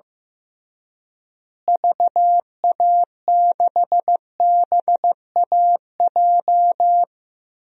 Morse code, heard as a keyed tone, about 15 wpm, 700 Hz. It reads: VA6BAJ
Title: DXCC-100 Call Signs - 6 Characters - Repeated 3x - 15wpm